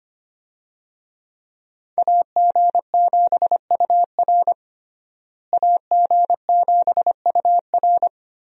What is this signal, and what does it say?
Morse code, heard as a keyed tone, about 25 wpm, 700 Hz. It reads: AG7UR AG7UR
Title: USA Call Signs - 5 Characters - Repeated 2x - 25wpm